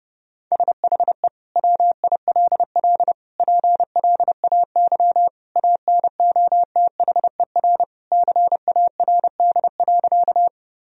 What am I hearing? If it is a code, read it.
Morse code, heard as a keyed tone, about 30 wpm, 700 Hz. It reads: SHE WILL PLAY ANOTHER CARD.
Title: Sentences from Top 1000 Words - Review Part 4 - 30wpm